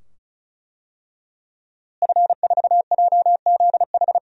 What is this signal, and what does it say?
Morse code, heard as a keyed tone, about 36 wpm, 700 Hz. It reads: F4JZH
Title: DXCC-100 Call Signs - 5 Characters - Repeated 2x - 35wpm